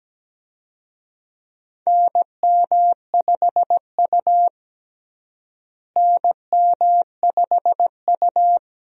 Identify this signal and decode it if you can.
Morse code, heard as a keyed tone, about 17 wpm, 700 Hz. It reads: NM5U NM5U
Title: USA Call Signs - 4 Characters - Repeated 3x - 17wpm